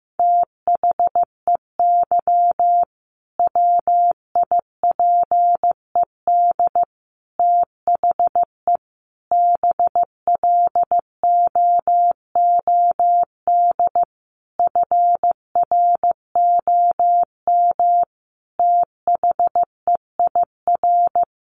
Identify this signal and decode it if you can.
Morse code, heard as a keyed tone, about 15 wpm, 700 Hz. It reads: THEY WIPED THE BLOOD FROM THEIR
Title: A Sound Of Thunder 15wpm